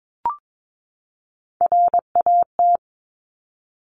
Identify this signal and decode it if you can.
Morse code, heard as a keyed tone, about 22 wpm, 700 Hz. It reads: RAT
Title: Top Three Letter Combinations in Words 22wpm